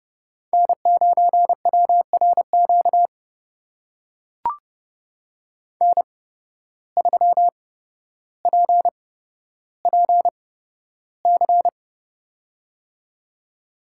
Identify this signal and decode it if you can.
Morse code, heard as a keyed tone, about 32 wpm, spaced as for 13 wpm, 700 Hz. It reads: N9WRQ N3PPC
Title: USA Call Signs - 5 Characters - 8x Character Spacing - 30wpm